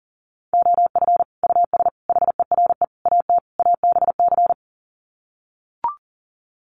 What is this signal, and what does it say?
Morse code, heard as a keyed tone, about 40 wpm, 700 Hz. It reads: OF US HERE AT ABC
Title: Sets of 5 Words 40wpm